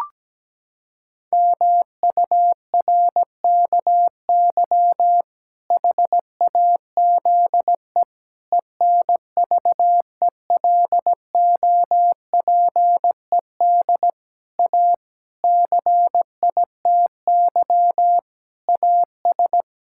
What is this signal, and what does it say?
Morse code, heard as a keyed tone, about 17 wpm, 700 Hz. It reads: MURKY HAZE ENVELOPED A CITY AS
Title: Pangram Challenge v2 - 17wpm